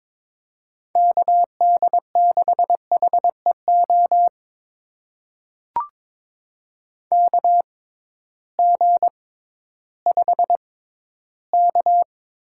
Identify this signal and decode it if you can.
Morse code, heard as a keyed tone, about 22 wpm, spaced as for 13 wpm, 700 Hz. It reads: KD6HEO KG5K
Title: USA Call Signs - 6 Characters - 6x Character Spacing - 22wpm